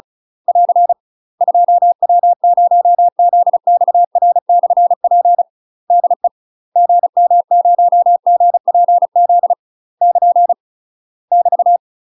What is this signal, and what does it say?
Morse code, heard as a keyed tone, about 36 wpm, 700 Hz. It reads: <AR> 2W0ZXR/P DE GM0GPZ <KN> <BT>
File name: UK MASTER QSOs TRAINING / 35WPM / UK MASTER QSOs 0033